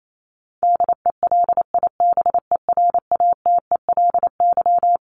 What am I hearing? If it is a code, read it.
Morse code, heard as a keyed tone, about 28 wpm, 700 Hz. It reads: DELIBERATELY
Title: Top 1751-2000 Words 28wpm